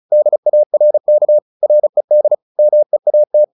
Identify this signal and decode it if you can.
Morse code, heard as a keyed tone, about 34 wpm, 600 Hz. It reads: DARK RED MEAT
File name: the old man and the sea / the old man and the sea 1 35wpm